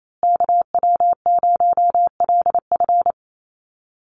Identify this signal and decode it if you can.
Morse code, heard as a keyed tone, about 28 wpm, 700 Hz. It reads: KW0LF
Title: USA Call Signs Repeated 3x 28wpm